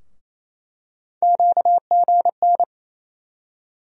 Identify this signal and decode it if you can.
Morse code, heard as a keyed tone, about 28 wpm, 700 Hz. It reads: QGN
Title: USA Call Sign Postfixes - 3 Letters 28wpm